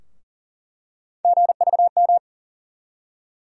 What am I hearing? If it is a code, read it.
Morse code, heard as a keyed tone, about 40 wpm, 700 Hz. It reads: GVM
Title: USA Call Sign Postfixes - 3 Letters 40wpm